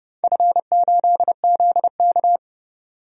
Morse code, heard as a keyed tone, about 30 wpm, 700 Hz. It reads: F8ZK
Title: DXCC-100 Call Signs - 4 Characters - 30wpm